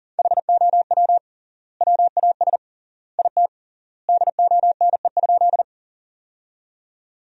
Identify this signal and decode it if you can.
Morse code, heard as a keyed tone, about 39 wpm, 700 Hz. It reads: HOW WAS IT DONE?
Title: 3x Word-Spacing - Sentences from Top 500 Words - 40wpm